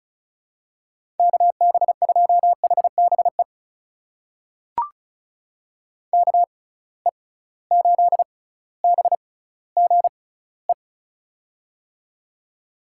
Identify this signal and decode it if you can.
Morse code, heard as a keyed tone, about 36 wpm, spaced as for 18 wpm, 700 Hz. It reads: KB2HBE KE8BGE
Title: USA Call Signs - 6 Characters - 6x Character Spacing - 35wpm